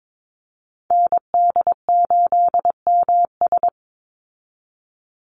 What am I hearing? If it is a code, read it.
Morse code, heard as a keyed tone, about 22 wpm, 700 Hz. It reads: ND8MS
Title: Call Signs Repeated 2x - 22wpm